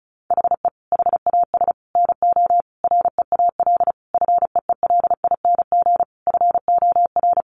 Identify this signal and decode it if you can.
Morse code, heard as a keyed tone, about 34 wpm, 700 Hz. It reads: HE HAS NO REAL FEELING FOR
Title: Sentences from Top 1200 Words 35wpm